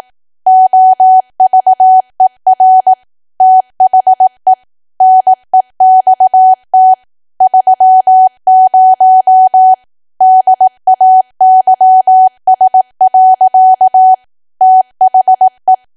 Morse code, heard as a keyed tone, about 18 wpm, 750 Hz. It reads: OVER THE NEXT 30 DAYS. THE
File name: cwa / W1AW-Level2 / 18WPM